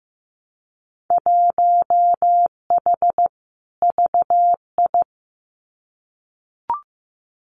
Morse code, heard as a keyed tone, about 15 wpm, 700 Hz. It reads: 1H VI
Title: Winter Field Day Exchanges Encore 15wpm